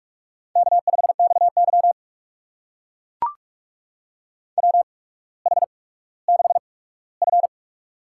Morse code, heard as a keyed tone, about 44 wpm, spaced as for 19 wpm, 700 Hz. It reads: K5XY WH6F
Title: USA Call Signs - 4 Characters - 8x Character Spacing - 45wpm